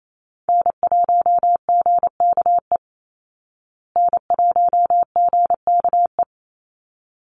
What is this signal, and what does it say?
Morse code, heard as a keyed tone, about 28 wpm, 700 Hz. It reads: N1GKE N1GKE
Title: Call Signs Repeated 3x - 28wpm